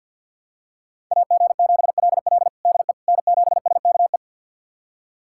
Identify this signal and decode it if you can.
Morse code, heard as a keyed tone, about 52 wpm, 690 Hz. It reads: AG7LL DE N7SKE
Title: USA Callsign DE Callsign - 5 Characters - 50wpm